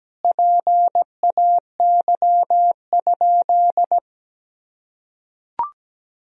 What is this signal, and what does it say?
Morse code, heard as a keyed tone, about 17 wpm, 700 Hz. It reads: PAY?
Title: Sentences from Top 1000 Words - Review Part 4 - 17wpm